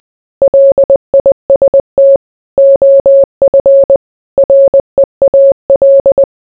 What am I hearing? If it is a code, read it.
Morse code, heard as a keyed tone, about 20 wpm, 550 Hz. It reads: LIST OF REAL